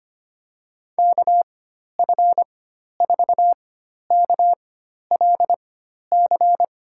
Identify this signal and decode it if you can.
Morse code, heard as a keyed tone, about 25 wpm, spaced as for 12 wpm, 700 Hz. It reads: KF4KLC
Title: USA Call Signs - 4x Character Spacing - 25wpm